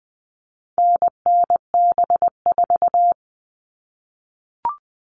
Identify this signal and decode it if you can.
Morse code, heard as a keyed tone, about 20 wpm, 700 Hz. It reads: NNB4
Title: All Cut Numbers - 4-Digit 20wpm